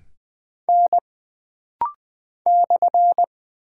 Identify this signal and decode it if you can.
Morse code, heard as a keyed tone, about 20 wpm, 700 Hz. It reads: N /
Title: Single Character (Letter:Number:Punctuation:Prosign) - Rapid-Fire - 20wpm